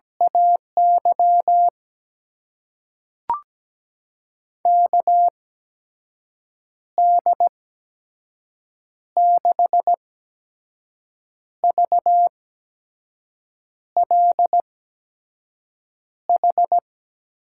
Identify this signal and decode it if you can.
Morse code, heard as a keyed tone, about 17 wpm, spaced as for 5 wpm, 700 Hz. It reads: AY KD6VLH
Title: USA Call Signs - 6 Characters - 8x Character Spacing - 17wpm